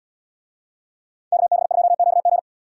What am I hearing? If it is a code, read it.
Morse code, heard as a keyed tone, about 73 wpm, 700 Hz. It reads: KB2QG